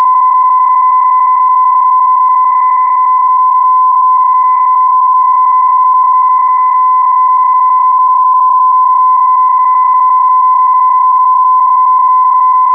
PSK31, about 31.25 baud.